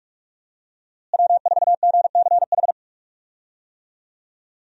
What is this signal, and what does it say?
Morse code, heard as a keyed tone, about 42 wpm, 700 Hz. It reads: W4GCH
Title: USA Call Signs - 5 Characters - Repeated 3x - 45wpm